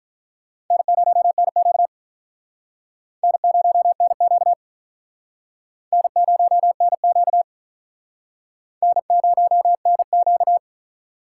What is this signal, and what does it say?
Morse code, heard as a keyed tone, about 45 wpm, 700 Hz. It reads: N0NQ N0NQ N0NQ N0NQ